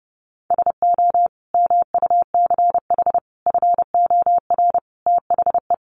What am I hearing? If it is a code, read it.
Morse code, heard as a keyed tone, about 30 wpm, 700 Hz. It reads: SO MUCH FOR THE
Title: Sentences from Top 1000 Words 30wpm